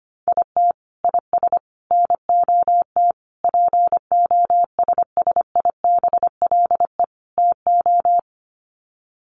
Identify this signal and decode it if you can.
Morse code, heard as a keyed tone, about 25 wpm, 700 Hz. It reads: IT IS NOT POSSIBLE TO
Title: Sets of 5 Words Encore2 25wpm